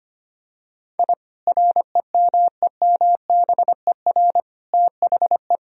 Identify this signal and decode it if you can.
Morse code, heard as a keyed tone, about 25 wpm, 700 Hz. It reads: I REMEMBER THE
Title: Sentences from Top 1000 Words - Review Part 2 - 25wpm